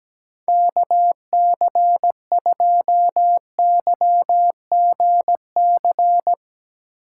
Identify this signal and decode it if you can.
Morse code, heard as a keyed tone, about 17 wpm, 700 Hz. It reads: KC2YGC
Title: USA Call Signs - 4x Character Spacing - 17wpm